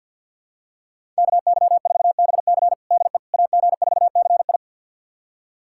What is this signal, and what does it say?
Morse code, heard as a keyed tone, about 52 wpm, 700 Hz. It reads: KY4BC DE AG4CI
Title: USA Callsign DE Callsign - 5 Characters - 50wpm